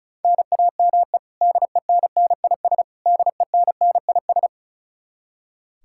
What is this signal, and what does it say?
Morse code, heard as a keyed tone, about 36 wpm, 700 Hz. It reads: NAME DENNIS DENNIS
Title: Practice QSO - Exchange 1 - Beginning - 35wpm